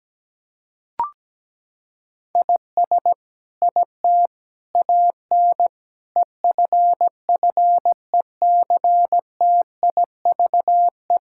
Morse code, heard as a keyed tone, about 17 wpm, 700 Hz. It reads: IS IT AN EFFECTIVE
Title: Sentences from Top 1000 Words - Review Part 6 - 17wpm